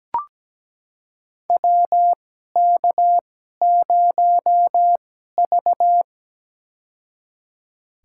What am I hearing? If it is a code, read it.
Morse code, heard as a keyed tone, about 17 wpm, spaced as for 12 wpm, 700 Hz. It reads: WK0V
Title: USA Call Signs - 4 Characters - 2x Character Spacing - 17wpm